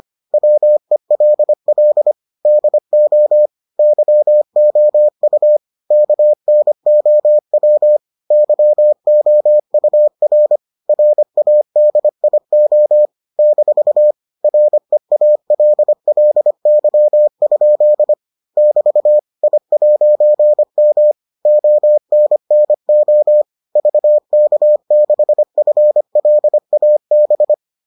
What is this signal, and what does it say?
Morse code, heard as a keyed tone, about 25 wpm, 600 Hz. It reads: WELL DO YOU KNOW YOUR RADIO - REALLY? <BT> I'M ONNO VK6FLAB